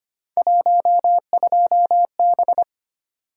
Morse code, heard as a keyed tone, about 25 wpm, 700 Hz. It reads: 12B
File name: Three Letter-Number Combinations - 25wpm